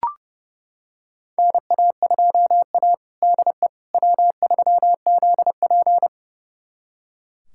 Morse code, heard as a keyed tone, about 30 wpm, 700 Hz. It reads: NA2A DE W3ZP
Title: USA Callsign DE Callsign - 4 Characters - 30wpm